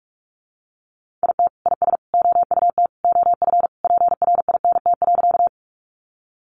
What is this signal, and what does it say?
Morse code, heard as a keyed tone, about 46 wpm, 700 Hz. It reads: IT IS OUT OF PRINT.
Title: Sentences from Top 1400 Words 45wpm